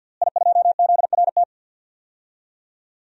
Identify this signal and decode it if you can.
Morse code, heard as a keyed tone, about 50 wpm, 700 Hz. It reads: I2ZRT
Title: DXCC-100 Call Signs - 5 Characters - Repeated 3x - 50wpm